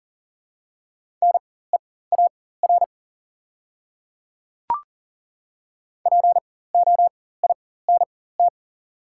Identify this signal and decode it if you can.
Morse code, heard as a keyed tone, about 40 wpm, spaced as for 19 wpm, 700 Hz. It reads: NEAR POINT
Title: 4x Character Spacing - Top 300 Words - 40wpm